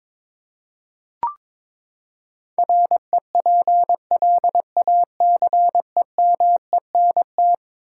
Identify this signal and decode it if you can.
Morse code, heard as a keyed tone, about 22 wpm, 700 Hz. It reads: REPLACEMENT
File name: practice / 22wpm / Eleven Letter Words - 22wpm